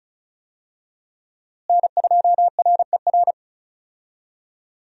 Morse code, heard as a keyed tone, about 34 wpm, 700 Hz. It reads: N2RER
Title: USA Call Signs - 5 Characters - Repeated 2x - 35wpm